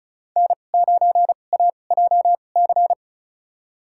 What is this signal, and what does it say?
Morse code, heard as a keyed tone, about 36 wpm, spaced as for 25 wpm, 700 Hz. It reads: N9AJC
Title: USA Call Signs - 5 Characters - 2x Character Spacing - 35wpm